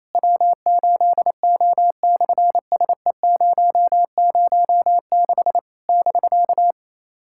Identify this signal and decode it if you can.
Morse code, heard as a keyed tone, about 27 wpm, 700 Hz. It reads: W8O/SE006 <BK>
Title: SOTA QSO Practice 28wpm